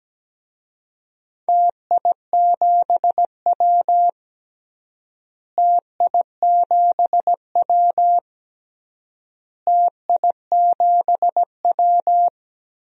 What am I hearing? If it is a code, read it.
Morse code, heard as a keyed tone, about 17 wpm, 700 Hz. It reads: TI7W TI7W TI7W
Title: DXCC-100 Call Signs - 4 Characters - Repeated 3x - 17wpm